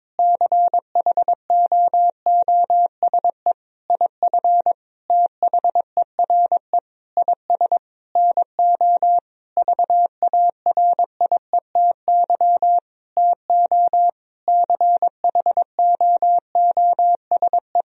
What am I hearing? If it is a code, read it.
Morse code, heard as a keyed tone, about 22 wpm, 700 Hz. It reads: CHOOSE IF THERE IS NO VARIETY TO CHOOSE
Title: Sentences from Top 1000 Words - Review Part 4 - 22wpm